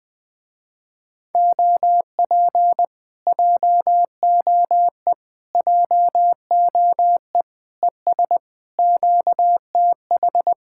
Morse code, heard as a keyed tone, about 21 wpm, 700 Hz. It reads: OP JOE JOE ES QTH
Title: Speed-Racing Common QSO Phrases - 20wpm